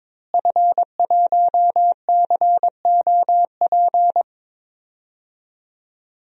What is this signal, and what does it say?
Morse code, heard as a keyed tone, about 22 wpm, 700 Hz. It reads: F1COP
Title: DXCC-100 Call Signs - 5 Characters - Repeated 3x - 22wpm